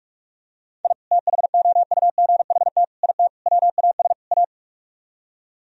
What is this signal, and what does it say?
Morse code, heard as a keyed tone, about 45 wpm, 700 Hz. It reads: I THOUGHT IT WAS A